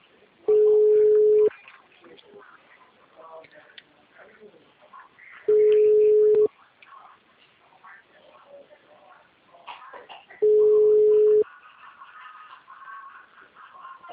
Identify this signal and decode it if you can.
Ringback tone (Europe, 425 Hz), with speech.